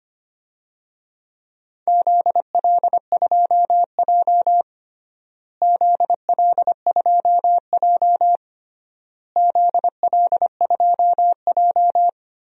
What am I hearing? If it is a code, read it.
Morse code, heard as a keyed tone, about 25 wpm, 700 Hz. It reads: ZL2J ZL2J ZL2J
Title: DXCC-100 Call Signs - 4 Characters - Repeated 3x - 25wpm